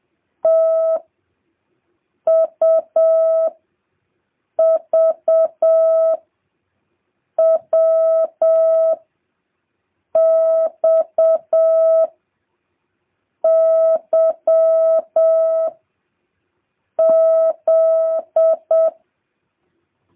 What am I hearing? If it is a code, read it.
Morse code, heard as a keyed tone, about 7 wpm, 650 Hz. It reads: T U V W X Y Z